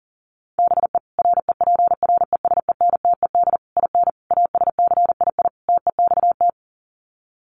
Morse code, heard as a keyed tone, about 40 wpm, 700 Hz. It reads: BE REPRESENTED IN ASCII TEXT
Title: Sets of 5 Words 40wpm